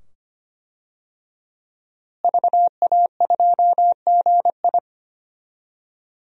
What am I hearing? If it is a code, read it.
Morse code, heard as a keyed tone, about 25 wpm, 700 Hz. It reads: VA2GI
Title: DXCC-100 Call Signs - 5 Characters - Repeated 2x - 25wpm